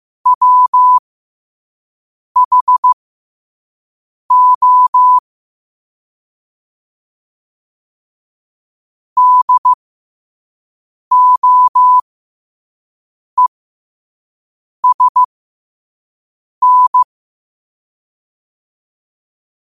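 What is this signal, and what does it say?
Morse code, heard as a keyed tone, about 15 wpm, spaced as for 5 wpm, 1000 Hz. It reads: WHO DOESN
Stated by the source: Morse code sent at 5 wpm.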